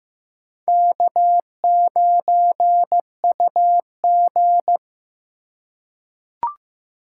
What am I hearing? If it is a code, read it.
Morse code, heard as a keyed tone, about 15 wpm, 700 Hz. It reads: K9UG
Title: USA Call Signs - 4 Characters - 2x Character Spacing - 15wpm